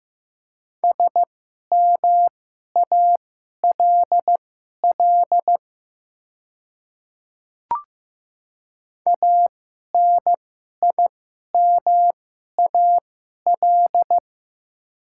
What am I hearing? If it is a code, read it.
Morse code, heard as a keyed tone, about 15 wpm, spaced as for 11 wpm, 690 Hz. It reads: SMALL ANIMAL
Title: 2x Character Spacing - Top 200 Words - 15wpm